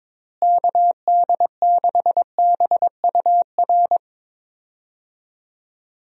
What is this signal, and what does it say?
Morse code, heard as a keyed tone, about 22 wpm, 700 Hz. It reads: KD6BUR
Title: USA Call Signs - 6 Characters - Repeated 2x - 22wpm